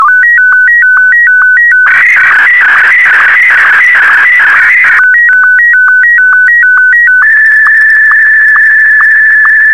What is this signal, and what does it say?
SSTV (slow-scan television), Martin 1.
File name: sstv / M1-VK4MSL